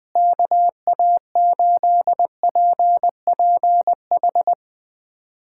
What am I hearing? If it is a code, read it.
Morse code, heard as a keyed tone, about 20 wpm, 700 Hz. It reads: KA8PPH
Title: USA Callsign DE Callsign - 6 Characters - 20wpm